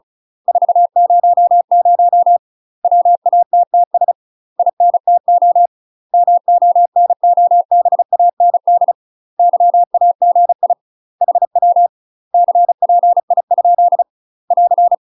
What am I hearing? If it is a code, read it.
Morse code, heard as a keyed tone, about 36 wpm, 700 Hz. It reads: 400 WATTS INTO MONOBAND YAGI HW CPI ? <AR>
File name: UK MASTER QSOs TRAINING / 35WPM / UK MASTER QSOs 0027